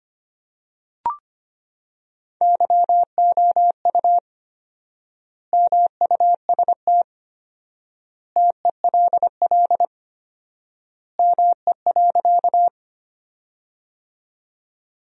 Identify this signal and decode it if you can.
Morse code, heard as a keyed tone, about 25 wpm, 700 Hz. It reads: YOU MUST TELL ME.
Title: 4x Word-Spacing - Sentences from Top 200 Words - 25wpm